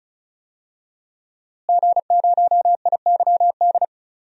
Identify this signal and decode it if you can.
Morse code, heard as a keyed tone, about 34 wpm, 700 Hz. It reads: G0IYD
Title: DXCC-100 Call Signs - 5 Characters - Repeated 3x - 35wpm